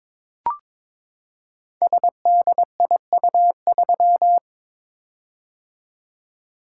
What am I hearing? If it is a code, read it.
Morse code, heard as a keyed tone, about 22 wpm, 700 Hz. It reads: SDIU3
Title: Code Groups - 5 Letter:Numbers - 22wpm